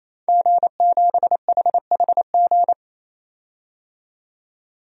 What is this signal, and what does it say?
Morse code, heard as a keyed tone, about 28 wpm, 700 Hz. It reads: G7HHG
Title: DXCC-100 Call Signs - 5 Characters - 28wpm